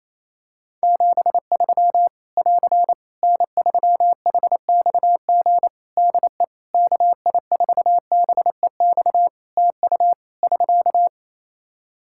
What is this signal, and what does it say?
Morse code, heard as a keyed tone, about 28 wpm, 700 Hz. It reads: 73 <AR> N3HXG DE KI4BEX TU <SK>
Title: K1USN SST QSO Practice 28wpm